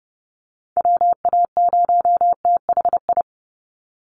Morse code, heard as a keyed tone, about 30 wpm, 700 Hz. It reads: WA0THI
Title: USA Call Signs 30wpm